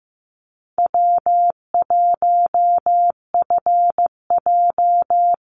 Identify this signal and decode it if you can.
Morse code, heard as a keyed tone, about 15 wpm, 700 Hz. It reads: W1FJ
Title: Easy Call Signs Repeated 3x - 15wpm